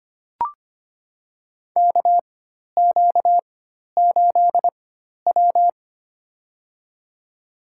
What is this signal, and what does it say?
Morse code, heard as a keyed tone, about 25 wpm, spaced as for 12 wpm, 700 Hz. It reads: KQ8W
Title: USA Call Signs - 4 Characters - 4x Character Spacing - 25wpm